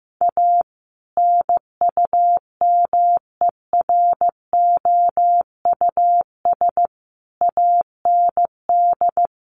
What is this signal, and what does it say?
Morse code, heard as a keyed tone, about 15 wpm, 700 Hz. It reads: A NUMEROUS AND
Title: Sentences from Top 1400 Words 15wpm